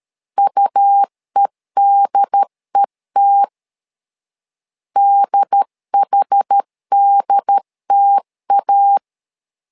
Morse code, heard as a keyed tone, about 13 wpm, 800 Hz. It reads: UEDET DHDTA
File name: LEZIONI / 2 UHDN / 2 UHD 12wpm 1540